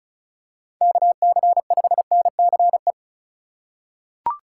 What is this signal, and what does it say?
Morse code, heard as a keyed tone, about 34 wpm, 700 Hz. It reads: KC5NCE
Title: USA Call Signs - 6 Characters - 2x Character Spacing - 35wpm